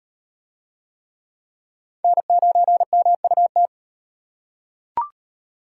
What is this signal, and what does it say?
Morse code, heard as a keyed tone, about 34 wpm, 700 Hz. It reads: N9MUT